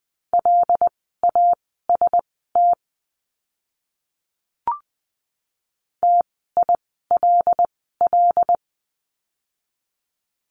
Morse code, heard as a keyed tone, about 20 wpm, spaced as for 14 wpm, 700 Hz. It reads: LAST TILL
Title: 075 Lesson 13f - Letter L - Words 20wpm